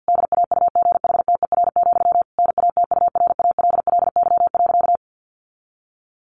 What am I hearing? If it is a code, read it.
Morse code, heard as a keyed tone, about 50 wpm, 700 Hz. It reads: DAUGHTER, NATURALLY.
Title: Sentences from Top 1500 Words 50wpm